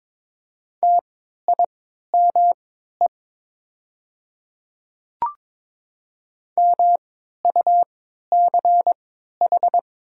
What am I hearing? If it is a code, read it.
Morse code, heard as a keyed tone, about 22 wpm, spaced as for 12 wpm, 690 Hz. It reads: TIME MUCH
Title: 3x Character Spacing - Top 300 Words - 22wpm